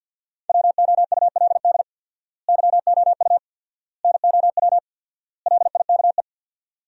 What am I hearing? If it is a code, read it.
Morse code, heard as a keyed tone, about 49 wpm, 700 Hz. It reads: WOULD YOU NOW LIKE
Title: 4x Word-Spacing - Sentences from Top 200 Words - 50wpm